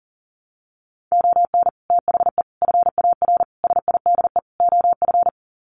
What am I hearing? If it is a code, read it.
Morse code, heard as a keyed tone, about 40 wpm, 700 Hz. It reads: ON THE FAR SIDE OF
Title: Sets of 5 Words 40wpm